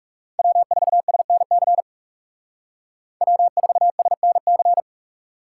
Morse code, heard as a keyed tone, about 35 wpm, 700 Hz. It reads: W4SNC W4SNC
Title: Speed Racing - USA 1x3 Call Signs 30wpm